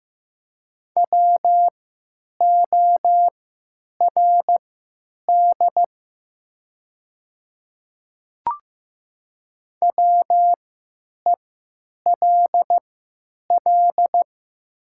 Morse code, heard as a keyed tone, about 15 wpm, spaced as for 9 wpm, 700 Hz. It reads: WORD WELL
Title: 3x Character Spacing - Top 500 Words - 15wpm